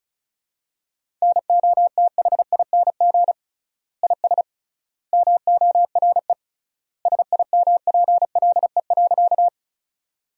Morse code, heard as a keyed tone, about 34 wpm, 700 Hz. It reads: NOTHING IS MORE SIMPLE.
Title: 3x Word-Spacing - Sentences from Top 500 Words - 35wpm